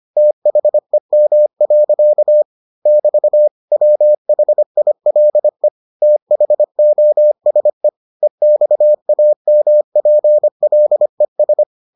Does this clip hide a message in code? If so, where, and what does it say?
Yes: Morse code from 0.2 to 11.6 s, heard as a keyed tone, about 25 wpm, 600 Hz. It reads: THEM. <BT> WHILE THOSE EXAMPLES